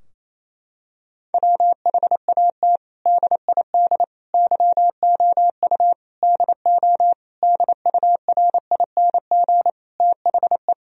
Morse code, heard as a keyed tone, about 28 wpm, 700 Hz. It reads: WHAT DID YOU DO DURING THE
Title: Sentences from Top 1000 Words - Review Part 2 - 28wpm